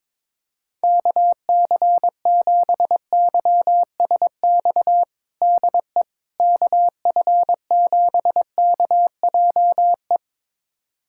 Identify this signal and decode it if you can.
Morse code, heard as a keyed tone, about 21 wpm, 700 Hz. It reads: KC7YSX DE KF7KJE
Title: USA Callsign DE Callsign - 6 Characters - 22wpm